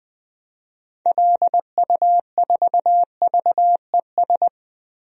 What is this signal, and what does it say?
Morse code, heard as a keyed tone, about 20 wpm, 700 Hz. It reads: LU4VES
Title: DXCC-100 Call Signs - 6 Characters - 20wpm